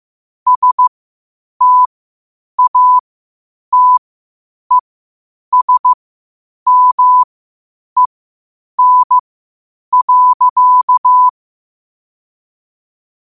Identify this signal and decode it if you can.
Morse code, heard as a keyed tone, about 15 wpm, spaced as for 8 wpm, 1000 Hz. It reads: STATESMEN.
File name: qotdfiles / 8wpm